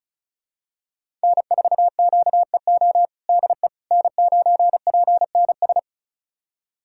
Morse code, heard as a keyed tone, about 36 wpm, 700 Hz. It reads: N4QEO DE N9PNS
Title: USA Callsign DE Callsign - 5 Characters - 35wpm